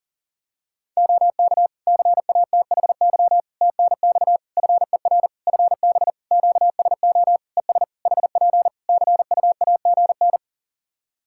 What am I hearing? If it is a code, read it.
Morse code, heard as a keyed tone, about 40 wpm, 700 Hz. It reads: OK CATHY TNX FER FB QSO ES HP CUAGN
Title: Practice QSO - Exchange 4 - Ending - 40wpm